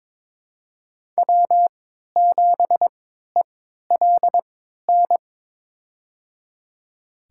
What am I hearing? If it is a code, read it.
Morse code, heard as a keyed tone, about 22 wpm, spaced as for 12 wpm, 700 Hz. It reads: W7ELN
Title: USA Call Signs - 3x Character Spacing - 22wpm